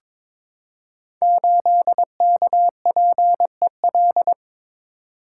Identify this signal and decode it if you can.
Morse code, heard as a keyed tone, about 22 wpm, 700 Hz. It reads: 8KPEL
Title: Code Groups - 5 Letter:Numbers - 22wpm